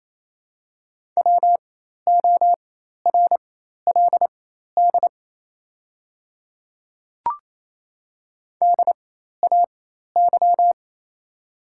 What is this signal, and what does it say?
Morse code, heard as a keyed tone, about 28 wpm, spaced as for 13 wpm, 690 Hz. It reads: WORLD DAY
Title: 4x Character Spacing - Top 200 Words - 28wpm